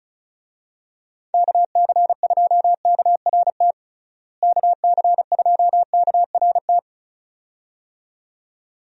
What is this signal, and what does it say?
Morse code, heard as a keyed tone, about 36 wpm, 700 Hz. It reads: KC2KRT KC2KRT
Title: USA Call Signs - 6 Characters - Repeated 2x - 35wpm